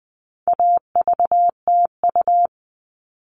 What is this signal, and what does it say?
Morse code, heard as a keyed tone, about 20 wpm, 700 Hz. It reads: AVTU
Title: All Cut Numbers - 4-Digit 20wpm